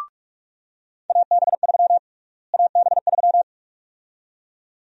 Morse code, heard as a keyed tone, about 47 wpm, 700 Hz. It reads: AB3 AB3
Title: USA Call Sign Prefixes - Repeated 2x - 45wpm